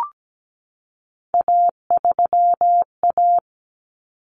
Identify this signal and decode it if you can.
Morse code, heard as a keyed tone, about 17 wpm, 700 Hz. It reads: A3A
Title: Common Cut Numbers - 3-Digit 17wpm